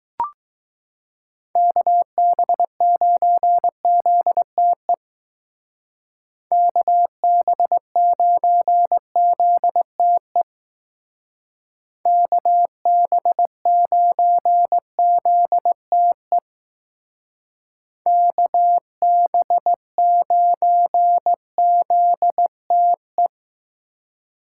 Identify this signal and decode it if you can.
Morse code, heard as a keyed tone, about 20 wpm, 700 Hz. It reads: KB9ZTE KB9ZTE KB9ZTE KB9ZTE